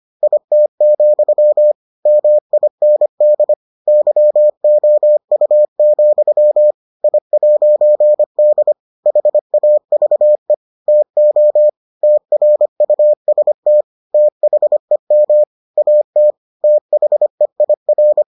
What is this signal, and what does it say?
Morse code, heard as a keyed tone, about 25 wpm, 600 Hz. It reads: IT, MIND YOU, I'D HAVE TO TRUST THEM AT THEIR